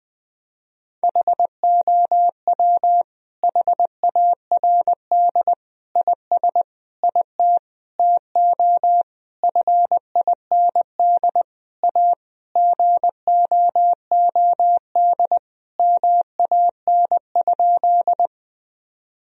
Morse code, heard as a keyed tone, about 21 wpm, 700 Hz. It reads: HOW HARD IS IT TO FIND A GOOD MAN?
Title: Sentences from Top 1000 Words - Review Part 1 - 20wpm